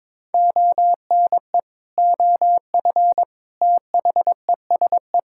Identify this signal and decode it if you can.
Morse code, heard as a keyed tone, about 23 wpm, 700 Hz. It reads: ONE OF THESE
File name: practice / 22wpm / Sentences with 8-Word Phrases - 22wpm